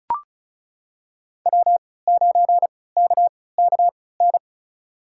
Morse code, heard as a keyed tone, about 34 wpm, spaced as for 20 wpm, 700 Hz. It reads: W9KKN
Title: USA Call Signs - 5 Characters - 3x Character Spacing - 35wpm